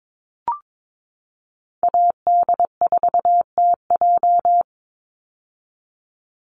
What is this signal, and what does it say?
Morse code, heard as a keyed tone, about 22 wpm, 700 Hz. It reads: AD4TJ
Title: Call Signs - 22wpm